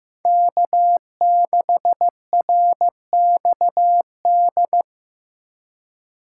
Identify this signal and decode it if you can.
Morse code, heard as a keyed tone, about 18 wpm, 700 Hz. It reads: K6RXD
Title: Speed Racing - USA 1x3 Call Signs 15wpm